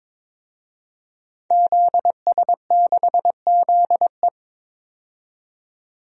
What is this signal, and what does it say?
Morse code, heard as a keyed tone, about 22 wpm, 700 Hz. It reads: ZS6ZE
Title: DXCC-100 Call Signs - 5 Characters - 22wpm